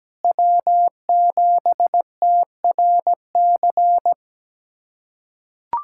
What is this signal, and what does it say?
Morse code, heard as a keyed tone, about 17 wpm, 700 Hz. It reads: W7TRC
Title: USA Callsign DE Callsign - 5 Characters - 17wpm